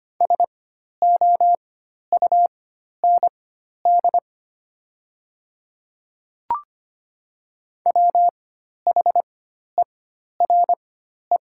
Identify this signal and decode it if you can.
Morse code, heard as a keyed tone, about 25 wpm, spaced as for 12 wpm, 690 Hz. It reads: SOUND WHERE
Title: 4x Character Spacing - Top 200 Words - 25wpm